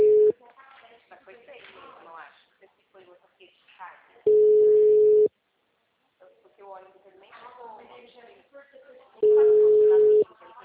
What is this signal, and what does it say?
Ringback tone (Europe, 425 Hz), with speech.
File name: out- -2014- - - .26851